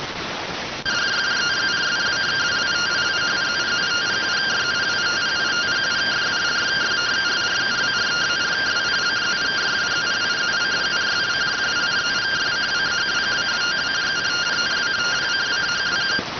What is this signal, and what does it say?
Two-tone teleprinter signal, about 170 Hz shift.